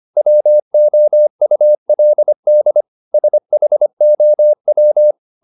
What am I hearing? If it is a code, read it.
Morse code, heard as a keyed tone, about 25 wpm, 600 Hz. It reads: WOULD SHOW
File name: the old man and the sea part 1 25wpm